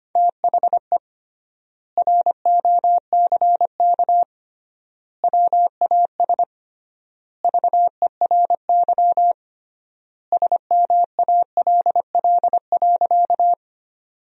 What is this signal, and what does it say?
Morse code, heard as a keyed tone, about 25 wpm, 700 Hz. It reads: THE ROCK WAS VERY SMALL.
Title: 3x Word-Spacing - Sentences from Top 400 Words - 25wpm